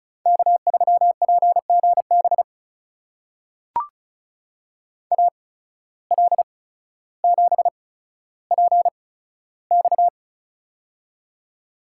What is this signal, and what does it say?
Morse code, heard as a keyed tone, about 36 wpm, spaced as for 15 wpm, 700 Hz. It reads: K3PGB AL7PX
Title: USA Call Signs - 5 Characters - 8x Character Spacing - 35wpm